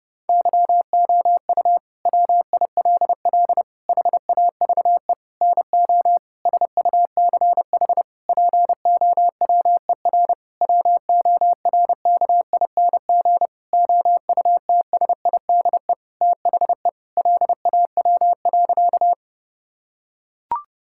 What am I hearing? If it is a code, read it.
Morse code, heard as a keyed tone, about 30 wpm, 700 Hz. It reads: YOU WILL HAVE NO SUCH POWER WORKING OUTSIDE THE LAW.
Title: Sentences from Top 1000 Words - Review Part 6 - 30wpm